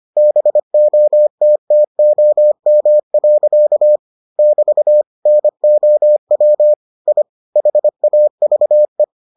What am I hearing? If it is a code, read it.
Morse code, heard as a keyed tone, about 25 wpm, 600 Hz. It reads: BOTTOM. <BT> NOW I HAVE